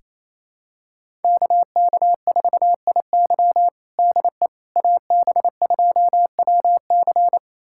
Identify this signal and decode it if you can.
Morse code, heard as a keyed tone, about 28 wpm, 700 Hz. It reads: KK4IY DE AB2WC
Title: USA Callsign DE Callsign - 5 Characters - 28wpm